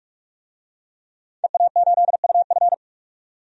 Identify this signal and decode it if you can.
Morse code, heard as a keyed tone, about 46 wpm, 700 Hz. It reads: EA8UF
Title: DXCC-100 Call Signs - 5 Characters - 45wpm